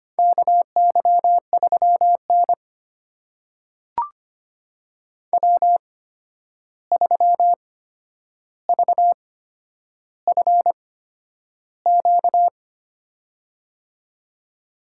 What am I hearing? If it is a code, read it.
Morse code, heard as a keyed tone, about 25 wpm, spaced as for 10 wpm, 700 Hz. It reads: KY3N W3VFQ
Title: USA Call Signs - 8x Character Spacing - 25wpm